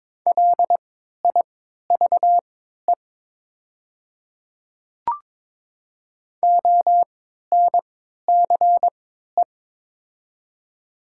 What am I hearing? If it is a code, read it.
Morse code, heard as a keyed tone, about 22 wpm, spaced as for 12 wpm, 700 Hz. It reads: LIVE ONCE
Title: 3x Character Spacing - Top 300 Words - 22wpm